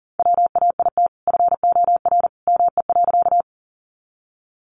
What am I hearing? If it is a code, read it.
Morse code, heard as a keyed tone, about 40 wpm, 700 Hz. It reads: WAIT FOR ME.
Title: Sentences from Top 500 Words 40wpm